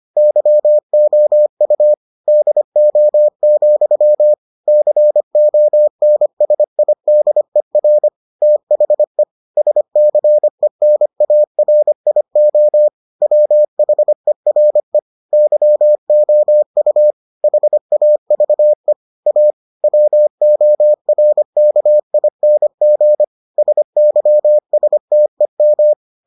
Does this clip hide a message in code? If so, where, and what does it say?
Yes: Morse code from 0.2 to 25.9 s, heard as a keyed tone, about 25 wpm, 600 Hz. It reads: YOU DO, CONSIDER THE SCENARIO WHERE YOU HAVE A WORKING SYSTEM